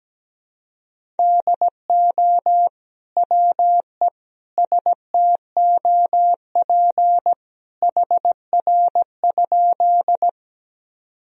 Morse code, heard as a keyed tone, about 17 wpm, 700 Hz. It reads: DO WE STOP HR?
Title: Sentences from Top 300 Words and Abbreviations 17wpm